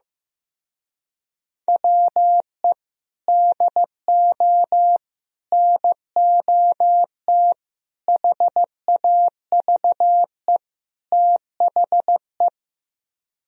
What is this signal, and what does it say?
Morse code, heard as a keyed tone, about 15 wpm, 700 Hz. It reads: WE DO NOT HAVE THE
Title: Speed-Racing - Sets of 5 Words 15wpm